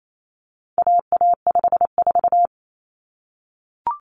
Morse code, heard as a keyed tone, about 28 wpm, 700 Hz. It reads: AA54
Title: Common Cut Numbers - 4-Digit 28wpm